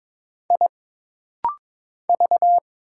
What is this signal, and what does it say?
Morse code, heard as a keyed tone, about 22 wpm, 700 Hz. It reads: I V
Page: This rapid-fire timing is geared to push your ICR (Instant Character Recognition) skills.